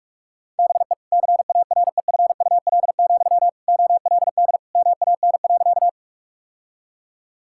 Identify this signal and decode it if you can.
Morse code, heard as a keyed tone, about 46 wpm, 700 Hz. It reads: BE CAREFUL, OLD MAN.
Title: Sentences from Top 1600 Words 45wpm